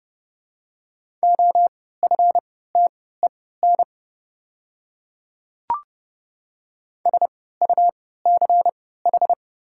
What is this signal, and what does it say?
Morse code, heard as a keyed tone, about 29 wpm, spaced as for 17 wpm, 700 Hz. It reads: OFTEN SUCH
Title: 3x Character Spacing - Top 500 Words - 30wpm